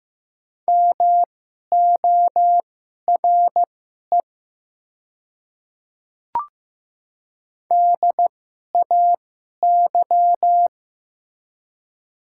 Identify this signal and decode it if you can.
Morse code, heard as a keyed tone, about 15 wpm, spaced as for 11 wpm, 700 Hz. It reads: MORE DAY
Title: 2x Character Spacing - Top 100 Words - 15wpm